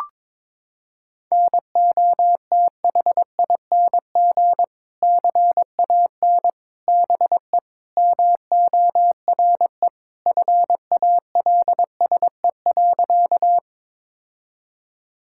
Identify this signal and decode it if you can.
Morse code, heard as a keyed tone, about 22 wpm, 700 Hz. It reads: NOTHING CAN BE MORE FALSE.
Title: Sentences from Top 1600 Words 22wpm